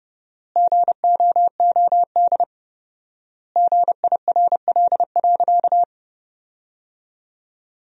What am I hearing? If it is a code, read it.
Morse code, heard as a keyed tone, about 30 wpm, 700 Hz. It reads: GOOD GIRL.
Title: 4x Word-Spacing - Sentences from Top 400 Words - 30wpm